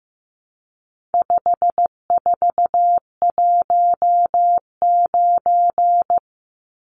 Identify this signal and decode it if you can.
Morse code, heard as a keyed tone, about 15 wpm, 700 Hz. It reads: 5419
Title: Numbers - 4-Digit 15wpm